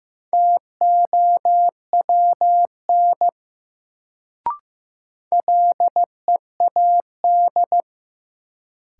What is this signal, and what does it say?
Morse code, heard as a keyed tone, about 15 wpm, 700 Hz. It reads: TOWN LEAD
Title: Sentences from Top 500 Words - New Word Drill - 15wpm